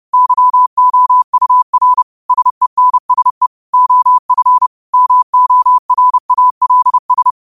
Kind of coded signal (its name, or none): Morse code